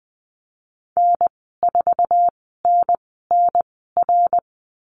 Morse code, heard as a keyed tone, about 20 wpm, spaced as for 15 wpm, 700 Hz. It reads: N4NNR
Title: 088 Lesson 15g - Number 5 - Call Signs 20wpm